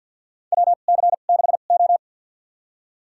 Morse code, heard as a keyed tone, about 47 wpm, spaced as for 35 wpm, 700 Hz. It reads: WC6O
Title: USA Call Signs - 4 Characters - 2x Character Spacing - 50wpm